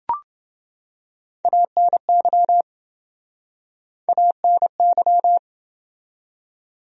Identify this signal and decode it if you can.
Morse code, heard as a keyed tone, about 29 wpm, 700 Hz. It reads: ANY ANY